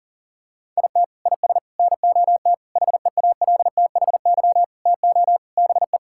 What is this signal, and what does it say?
Morse code, heard as a keyed tone, about 37 wpm, 700 Hz. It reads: IT IS NOT HEALTHY TO BE
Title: Sentences from Top 1000 Words - Review Part 6 - 40wpm